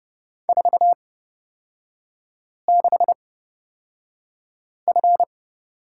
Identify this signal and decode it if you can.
Morse code, heard as a keyed tone, about 30 wpm, spaced as for 5 wpm, 700 Hz. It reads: 46F
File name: course / 30:10wpm / 236 Lesson 38e - Letter q - Three Characters 30wpm